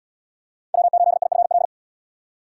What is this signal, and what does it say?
Morse code, heard as a keyed tone, about 66 wpm, 700 Hz. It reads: W8IFL